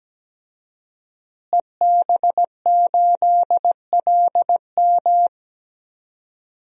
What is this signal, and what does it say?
Morse code, heard as a keyed tone, about 17 wpm, 700 Hz. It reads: EB8LM
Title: DXCC-100 Call Signs - 5 Characters - Repeated 3x - 17wpm